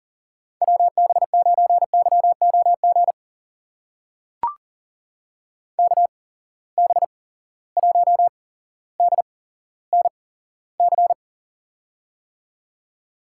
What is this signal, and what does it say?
Morse code, heard as a keyed tone, about 40 wpm, spaced as for 17 wpm, 700 Hz. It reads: WB9YOG KB1DNC
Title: USA Call Signs - 6 Characters - 8x Character Spacing - 40wpm